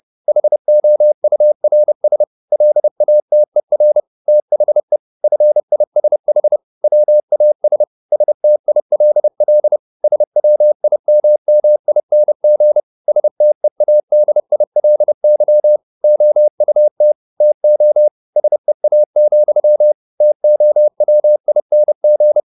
Morse code, heard as a keyed tone, about 30 wpm, 600 Hz. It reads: HOURS LATER THE FISH WAS STILL SWIMMING STEADILY OUT TO SEA, TOWING
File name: the old man and the sea / the old man and the sea 1 30wpm